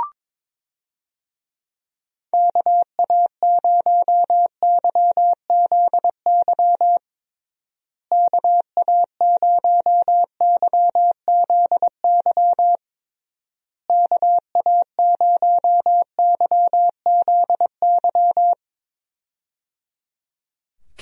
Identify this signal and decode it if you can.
Morse code, heard as a keyed tone, about 22 wpm, 700 Hz. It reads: KA0YZY KA0YZY KA0YZY
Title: USA Call Signs - 6 Characters - Repeated 3x - 22wpm